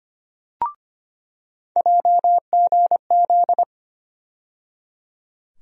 Morse code, heard as a keyed tone, about 25 wpm, 700 Hz. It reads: JGZ
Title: USA Call Sign Postfixes - 3 Letters 25wpm